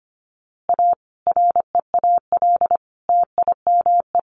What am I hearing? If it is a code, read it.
Morse code, heard as a keyed tone, about 25 wpm, 700 Hz. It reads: A REAL TIME
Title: A Sound Of Thunder 25wpm